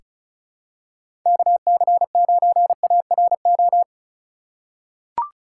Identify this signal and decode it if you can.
Morse code, heard as a keyed tone, about 36 wpm, 690 Hz. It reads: KC9ARO
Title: USA Call Signs - 3x Character Spacing - 35wpm